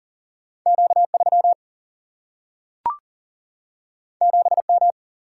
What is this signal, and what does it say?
Morse code, heard as a keyed tone, about 40 wpm, 700 Hz. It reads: Q3 7M
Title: Single-Letter Single-Number 40wpm